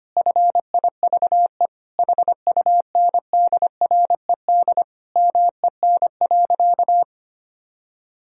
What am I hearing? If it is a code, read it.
Morse code, heard as a keyed tone, about 25 wpm, 700 Hz. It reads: FIVE HUNDRED MEN.
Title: Sentences from Top 1000 Words - Review Part 2 - 25wpm